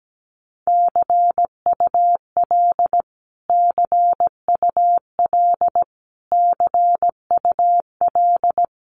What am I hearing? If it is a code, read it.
Morse code, heard as a keyed tone, about 17 wpm, 700 Hz. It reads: CUL CUL CUL
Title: Abbreviations Repeated 3x - 17wpm